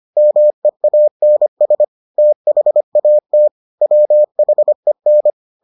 Morse code, heard as a keyed tone, about 25 wpm, 600 Hz. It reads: MEANS THAT WHEN